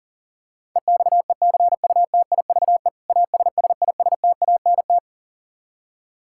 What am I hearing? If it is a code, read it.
Morse code, heard as a keyed tone, about 37 wpm, 700 Hz. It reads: EXECUTIVE ASSISTANT
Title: Practice QSO - Exchange 3 - More Info - 40wpm